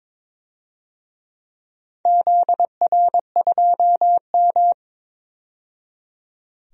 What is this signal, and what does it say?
Morse code, heard as a keyed tone, about 22 wpm, 700 Hz. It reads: ZR2M
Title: DXCC-100 Call Signs - 4 Characters - 22wpm